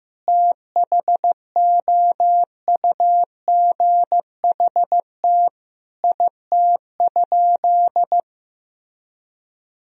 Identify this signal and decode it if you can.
Morse code, heard as a keyed tone, about 16 wpm, 700 Hz. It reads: THOUGHT IT?
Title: Sentences from Top 300 Words and Abbreviations 15wpm